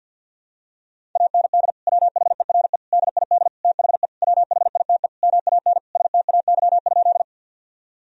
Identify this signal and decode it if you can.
Morse code, heard as a keyed tone, about 52 wpm, 700 Hz. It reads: AND WHERE DID THE WHITE MAN STAY?
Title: Sentences from Top 1000 Words - Review Part 1 - 50wpm